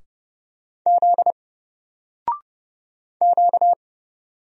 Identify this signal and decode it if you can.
Morse code, heard as a keyed tone, about 31 wpm, 700 Hz. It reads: Z Q
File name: Single-Letter-Number-30wpm